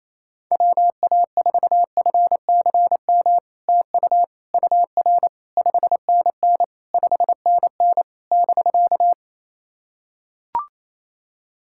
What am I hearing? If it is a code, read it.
Morse code, heard as a keyed tone, about 28 wpm, 700 Hz. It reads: WA4FCM TU UR 5NN 5NN <BK>
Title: SOTA QSO Practice 28wpm